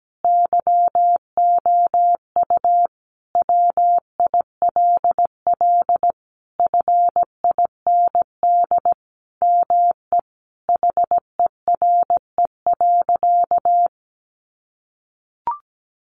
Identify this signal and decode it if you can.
Morse code, heard as a keyed tone, about 17 wpm, 700 Hz. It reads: YOU WILL FIND ME HERE.
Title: Sentences from Top 1200 Words 17wpm